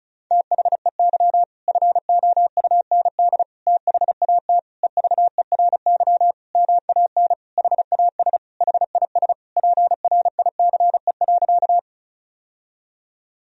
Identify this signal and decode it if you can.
Morse code, heard as a keyed tone, about 36 wpm, 700 Hz. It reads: THEY FOUND THAT EVERY MAN HAS HIS PRICE.
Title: Sentences from Top 1000 Words - Review Part 4 - 35wpm